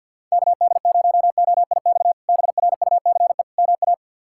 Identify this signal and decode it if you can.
Morse code, heard as a keyed tone, about 52 wpm, 700 Hz. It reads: KD0OIX BRUCE MA
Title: K1USN SST QSO Practice - 50wpm